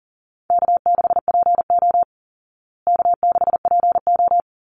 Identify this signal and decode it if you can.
Morse code, heard as a keyed tone, about 40 wpm, 700 Hz. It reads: K6PO K6PO
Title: Call Signs Repeated 3x - 40wpm